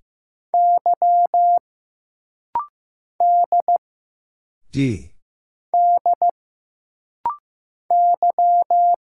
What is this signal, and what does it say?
Morse code, heard as a keyed tone, about 15 wpm, 700 Hz. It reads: Y D D Y
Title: Single Letter-Number Alt - Rapid-Fire 15wpm